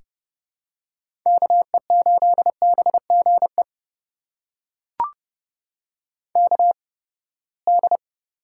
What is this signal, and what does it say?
Morse code, heard as a keyed tone, about 30 wpm, spaced as for 21 wpm, 700 Hz. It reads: KE8BGE KD
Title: USA Call Signs - 6 Characters - 8x Character Spacing - 30wpm